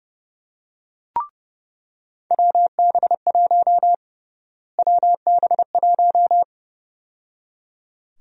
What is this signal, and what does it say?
Morse code, heard as a keyed tone, about 30 wpm, 700 Hz. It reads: WB1 WB1
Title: USA Call Sign Prefixes - Repeated 2x - 30wpm